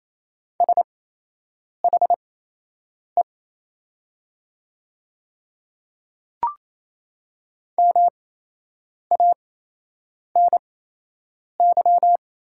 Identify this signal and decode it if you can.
Morse code, heard as a keyed tone, about 28 wpm, spaced as for 8 wpm, 700 Hz. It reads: SHE MANY
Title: 8x Character Spacing - Top 200 Words - 28wpm